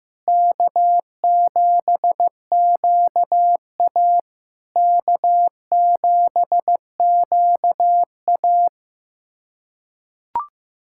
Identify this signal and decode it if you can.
Morse code, heard as a keyed tone, about 15 wpm, 700 Hz. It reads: K7QA K7QA
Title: SOTA QSO Practice 15wpm